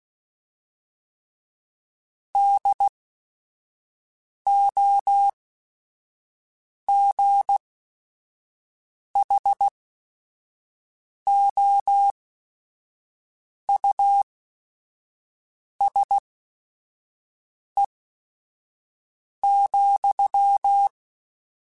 Morse code, heard as a keyed tone, about 16 wpm, spaced as for 5 wpm, 800 Hz. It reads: DOGHOUSE,